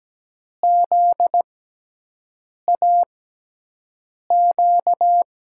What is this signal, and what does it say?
Morse code, heard as a keyed tone, about 17 wpm, spaced as for 6 wpm, 700 Hz. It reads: ZAQ
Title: USA Call Signs - 6 Characters - 6x Character Spacing - 17wpm